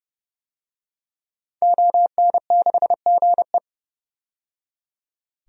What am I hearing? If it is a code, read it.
Morse code, heard as a keyed tone, about 32 wpm, 700 Hz. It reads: ON6GE
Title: DXCC-100 Call Signs - 5 Characters - 30wpm